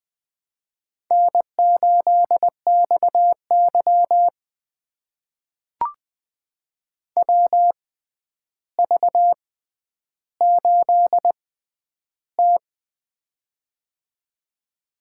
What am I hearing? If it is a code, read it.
Morse code, heard as a keyed tone, about 20 wpm, spaced as for 10 wpm, 700 Hz. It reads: N8XY WV8T
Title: USA Call Signs - 4 Characters - 6x Character Spacing - 20wpm